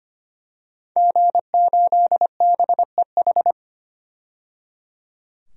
Morse code, heard as a keyed tone, about 25 wpm, 700 Hz. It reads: G8BEH
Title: DXCC-100 Call Signs - 5 Characters - 25wpm